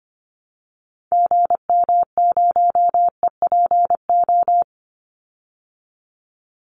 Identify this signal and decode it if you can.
Morse code, heard as a keyed tone, about 25 wpm, 700 Hz. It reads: GM0EPO
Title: Call Signs Repeated 2x - 25wpm